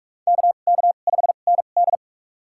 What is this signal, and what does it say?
Morse code, heard as a keyed tone, about 47 wpm, spaced as for 33 wpm, 690 Hz. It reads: KK5ND
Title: USA Call Signs - 5 Characters - 2x Character Spacing - 45wpm